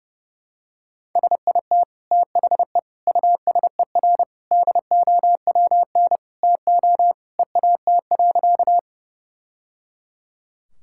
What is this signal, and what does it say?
Morse code, heard as a keyed tone, about 30 wpm, 700 Hz. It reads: SIT THE USER DOWN TO EAT.
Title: Sentences from Top 1000 Words - Review Part 5 - 30wpm